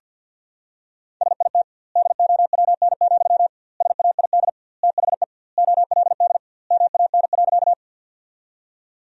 Morse code, heard as a keyed tone, about 48 wpm, 700 Hz. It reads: SIT DOWN, SAID THE OLD MAN.
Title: 2x Word-Spacing - Sentences from Top 500 Words - 50wpm